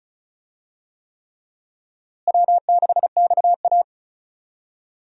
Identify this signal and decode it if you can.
Morse code, heard as a keyed tone, about 36 wpm, 700 Hz. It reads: W6XA
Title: USA Call Signs - 4 Characters - Repeated 3x - 35wpm